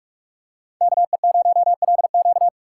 Morse code, heard as a keyed tone, about 43 wpm, 700 Hz. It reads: KE0LQ
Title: USA Call Signs - 5 Characters - 2x Character Spacing - 45wpm